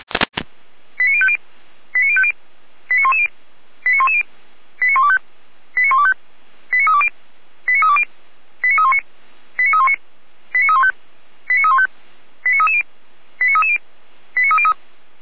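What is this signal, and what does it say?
Five-tone selective call (ZVEI), heard as quick runs of five tones, tones of about 70 ms. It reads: 89050 89050 89110 89110 89125 89125 89320 89320 89319 89319 89315 89315 89330 89330 89303
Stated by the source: ZVEI five-tone selective call received via SDR.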